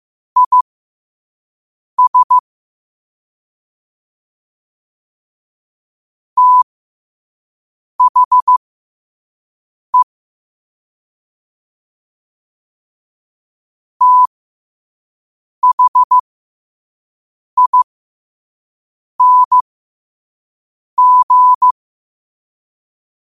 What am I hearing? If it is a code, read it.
Morse code, heard as a keyed tone, about 15 wpm, spaced as for 5 wpm, 1000 Hz. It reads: IS THE THING
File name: qotdfiles / 5wpm